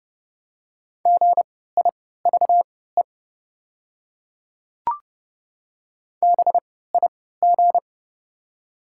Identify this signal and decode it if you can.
Morse code, heard as a keyed tone, about 30 wpm, spaced as for 17 wpm, 690 Hz. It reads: GIVE BIG
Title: 3x Character Spacing - Top 200 Words - 30wpm